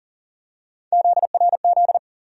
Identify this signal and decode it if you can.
Morse code, heard as a keyed tone, about 40 wpm, 700 Hz. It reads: ZRZ
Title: USA Call Sign Postfixes - 3 Letters 40wpm